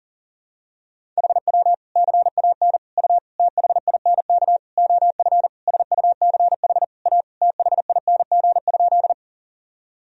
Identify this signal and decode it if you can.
Morse code, heard as a keyed tone, about 40 wpm, 700 Hz. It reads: HW CAN U THINK OF SUCH A THING?
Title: Sentences from Top 200 Words and Abbreviations 40wpm